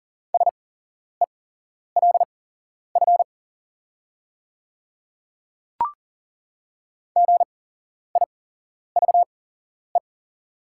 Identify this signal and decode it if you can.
Morse code, heard as a keyed tone, about 40 wpm, spaced as for 11 wpm, 700 Hz. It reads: SELF GIVE
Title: 8x Character Spacing - Top 200 Words - 40wpm